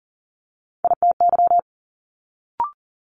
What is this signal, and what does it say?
Morse code, heard as a keyed tone, about 37 wpm, 700 Hz. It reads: ITY
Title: Multi-letter Phonemes Quick Fire 40wpm